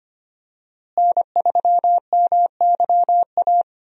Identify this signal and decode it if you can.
Morse code, heard as a keyed tone, about 25 wpm, 700 Hz. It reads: N3MYA
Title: USA Call Signs - 5 Characters - Repeated 2x - 25wpm